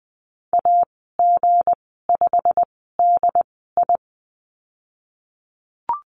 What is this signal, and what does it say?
Morse code, heard as a keyed tone, about 20 wpm, spaced as for 14 wpm, 700 Hz. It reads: AG5DI
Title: 179 Lesson 29g - Slash - Call Signs 20wpm